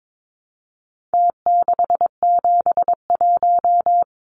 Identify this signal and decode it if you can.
Morse code, heard as a keyed tone, about 22 wpm, 700 Hz. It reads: T671
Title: Numbers - Contest Style - 4-Digit 22wpm